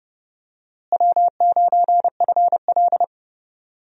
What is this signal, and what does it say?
Morse code, heard as a keyed tone, about 30 wpm, 700 Hz. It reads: W9FL
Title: USA Call Signs - 4 Characters - Repeated 2x - 30wpm